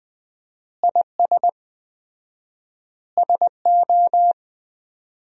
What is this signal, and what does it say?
Morse code, heard as a keyed tone, about 20 wpm, 700 Hz. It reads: IS SO
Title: 4x Word-Spacing - Sentences from Top 500 Words - 20wpm